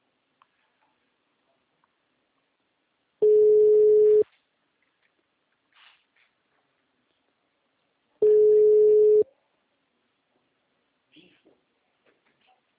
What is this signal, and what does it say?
Ringback tone (Europe, 425 Hz), with speech.